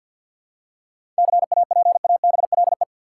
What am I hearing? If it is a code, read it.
Morse code, heard as a keyed tone, about 47 wpm, 700 Hz. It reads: CAPABLE
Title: Seven Letter Words - 50wpm